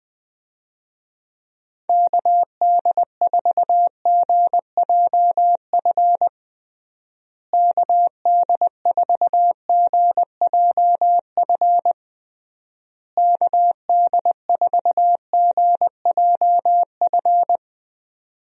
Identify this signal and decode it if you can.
Morse code, heard as a keyed tone, about 20 wpm, 700 Hz. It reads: KD4GJF KD4GJF KD4GJF
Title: USA Call Signs - 6 Characters - Repeated 3x - 20wpm